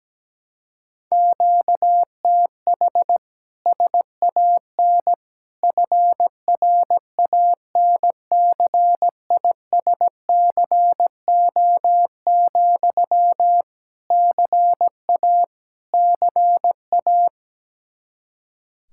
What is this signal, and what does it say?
Morse code, heard as a keyed tone, about 17 wpm, 700 Hz. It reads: QTH SAN FRANCISCO, CA CA
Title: K1USN SST QSO Practice 17wpm